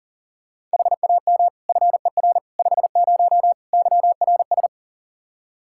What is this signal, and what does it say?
Morse code, heard as a keyed tone, about 40 wpm, 700 Hz. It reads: HAM FER 50 YRS
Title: Practice QSO - Exchange 3 - More Info - 40wpm